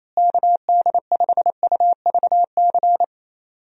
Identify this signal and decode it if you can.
Morse code, heard as a keyed tone, about 27 wpm, 700 Hz. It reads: KD5UVC
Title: USA Call Signs - 6 Characters - Repeated 3x - 28wpm